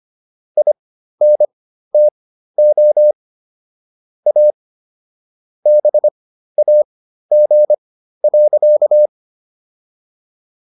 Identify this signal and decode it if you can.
Morse code, heard as a keyed tone, about 25 wpm, spaced as for 13 wpm, 600 Hz. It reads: INTO A BAG.